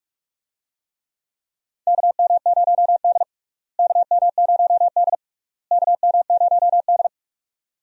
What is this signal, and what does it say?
Morse code, heard as a keyed tone, about 41 wpm, 700 Hz. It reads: KM0D KM0D KM0D
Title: USA Call Signs - 4 Characters - Repeated 3x - 45wpm